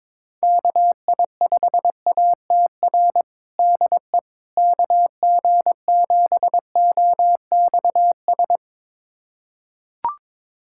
Morse code, heard as a keyed tone, about 22 wpm, 700 Hz. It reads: KI5ATR DE KG7OXS
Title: USA Callsign DE Callsign 22wpm